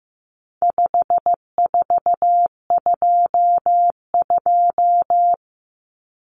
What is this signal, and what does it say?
Morse code, heard as a keyed tone, about 15 wpm, 700 Hz. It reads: 5422
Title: Numbers - 4-Digit 15wpm